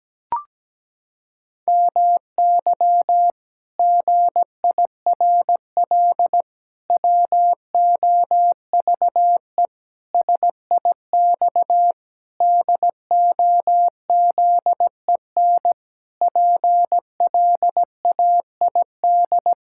Morse code, heard as a keyed tone, about 17 wpm, 700 Hz. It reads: MY GIRL WOVE SIX DOZEN PLAID
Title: Pangram Challenge v2 - 17wpm